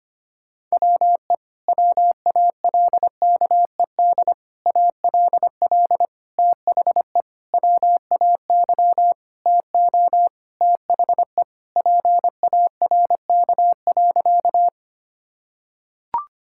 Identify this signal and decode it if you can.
Morse code, heard as a keyed tone, about 25 wpm, 700 Hz. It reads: WE WALKED ALL THE WAY TO THE PARK.
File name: practice / 25wpm / Sentences with 4-Word Phrases - 25wpm